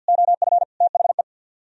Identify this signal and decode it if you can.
Morse code, heard as a keyed tone, about 58 wpm, 700 Hz. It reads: OF THE
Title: Speed-Racing - Sets of 3 Words 50wpm